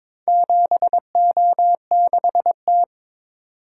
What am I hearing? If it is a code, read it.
Morse code, heard as a keyed tone, about 22 wpm, 700 Hz. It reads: 7O6T
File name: practice / 22wpm / Four Letter-Number Combinations - 22wpm